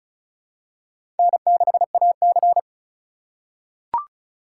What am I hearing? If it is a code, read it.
Morse code, heard as a keyed tone, about 36 wpm, 700 Hz. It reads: N6AC
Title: USA Call Signs - 4 Characters - 2x Character Spacing - 35wpm